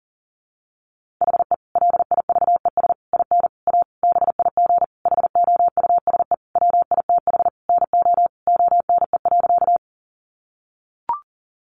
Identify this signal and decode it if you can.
Morse code, heard as a keyed tone, about 40 wpm, 700 Hz. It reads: HE LIVES IN A BIG HOUSE WITH NO ONE.
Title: Additional Easy Phrases 40wpm